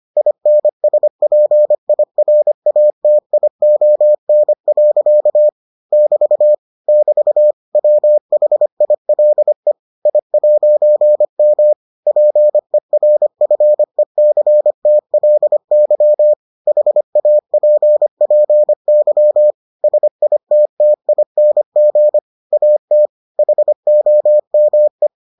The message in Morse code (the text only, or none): INSPIRATION. <BT> <BT> WHILE I'M PERFECTLY HAPPY SITTING AT HOME